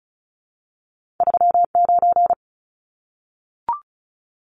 Morse code, heard as a keyed tone, about 36 wpm, 700 Hz. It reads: 39
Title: Numbers - 2-Digit 35wpm